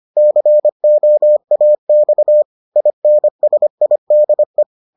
Morse code, heard as a keyed tone, about 25 wpm, 600 Hz. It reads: COAX INSIDE